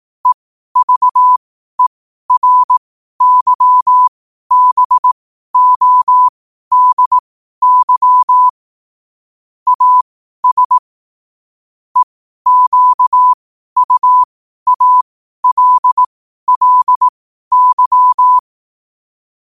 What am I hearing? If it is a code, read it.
Morse code, heard as a keyed tone, about 18 wpm, spaced as for 12 wpm, 1000 Hz. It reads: EVERYBODY AS EQUALLY